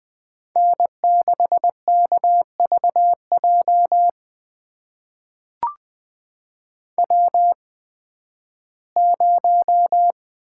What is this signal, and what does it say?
Morse code, heard as a keyed tone, about 20 wpm, spaced as for 13 wpm, 700 Hz. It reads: N6KVJ W0
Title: USA Call Signs - 5 Characters - 8x Character Spacing - 20wpm